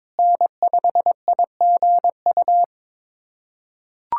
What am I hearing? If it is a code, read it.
Morse code, heard as a keyed tone, about 22 wpm, 700 Hz. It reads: N5IGU
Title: USA Callsign DE Callsign - 5 Characters - 22wpm